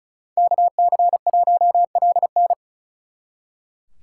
Morse code, heard as a keyed tone, about 45 wpm, 700 Hz. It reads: KC1LN